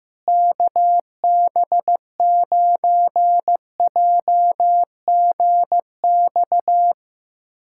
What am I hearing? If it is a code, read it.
Morse code, heard as a keyed tone, about 15 wpm, 700 Hz. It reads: KB9JGX
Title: USA Callsign DE Callsign 15wpm